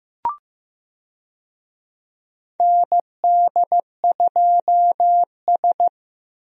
Morse code, heard as a keyed tone, about 15 wpm, 700 Hz. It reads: ND2S
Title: USA Call Signs - 4 Characters - Repeated 3x - 15wpm